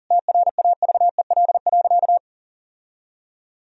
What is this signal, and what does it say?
Morse code, heard as a keyed tone, about 37 wpm, 700 Hz. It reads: TRAVEL.
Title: Sentences from Top 1000 Words - Review Part 2 - 40wpm